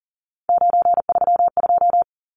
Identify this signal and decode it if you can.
Morse code, heard as a keyed tone, about 40 wpm, 700 Hz. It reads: 932
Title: Numbers - 3-Digit 40wpm